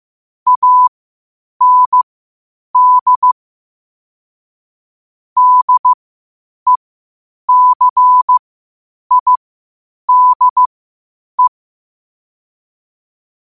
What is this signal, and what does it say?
Morse code, heard as a keyed tone, about 15 wpm, spaced as for 8 wpm, 1000 Hz. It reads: AND DECIDE
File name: qotdfiles / 8wpm